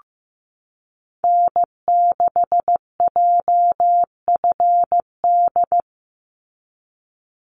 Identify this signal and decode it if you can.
Morse code, heard as a keyed tone, about 15 wpm, 700 Hz. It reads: N6JFD
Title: NA SOTA Activator Callsigns 15wpm